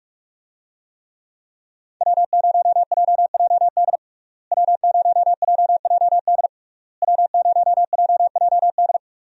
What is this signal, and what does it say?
Morse code, heard as a keyed tone, about 46 wpm, 700 Hz. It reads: W0JJD W0JJD W0JJD
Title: USA Call Signs - 5 Characters - Repeated 3x - 45wpm